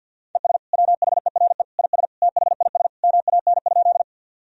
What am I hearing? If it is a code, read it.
Morse code, heard as a keyed tone, about 51 wpm, 700 Hz. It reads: ES WHERE IS THIS MAN?
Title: Sentences from Top 200 Words and Abbreviations 50wpm